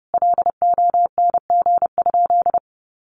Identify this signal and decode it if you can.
Morse code, heard as a keyed tone, about 30 wpm, 700 Hz. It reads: LONG?
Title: Sentences from Top 200 Words 30wpm